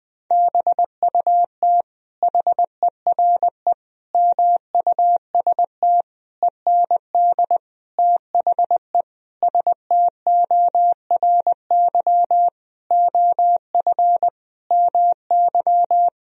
Morse code, heard as a keyed tone, about 20 wpm, 700 Hz. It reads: BUT HERE MUST END THE STORY OF MY
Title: Sentences from Top 1000 Words - Review Part 1 - 20wpm